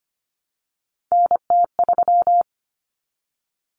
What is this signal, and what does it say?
Morse code, heard as a keyed tone, about 25 wpm, 700 Hz. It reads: NT3
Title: Common Cut Numbers - 3-Digit 25wpm